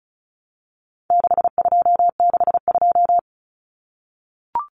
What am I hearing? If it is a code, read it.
Morse code, heard as a keyed tone, about 34 wpm, 700 Hz. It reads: 6262
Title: Numbers - 4-Digit 35wpm